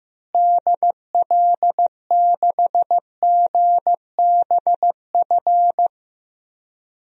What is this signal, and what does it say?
Morse code, heard as a keyed tone, about 15 wpm, 700 Hz. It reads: DL6GBF
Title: DXCC-100 Call Signs - 6 Characters - Repeated 2x - 15wpm